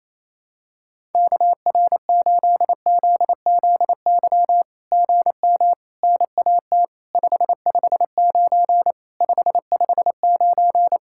Morse code, heard as a keyed tone, about 28 wpm, 700 Hz. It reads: KR8ZZY GM NAT 559 559
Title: SOTA QSO Practice 28wpm